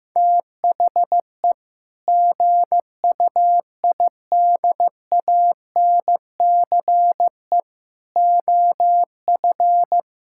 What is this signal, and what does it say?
Morse code, heard as a keyed tone, about 15 wpm, 700 Hz. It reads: THE GUIDANCE OF
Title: Sentences from Top 1600 Words 15wpm